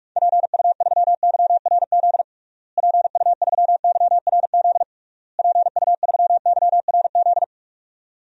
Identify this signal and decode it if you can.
Morse code, heard as a keyed tone, about 46 wpm, 700 Hz. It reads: PU3YRZ PU3YRZ PU3YRZ
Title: DXCC-100 Call Signs - 6 Characters - Repeated 3x - 45wpm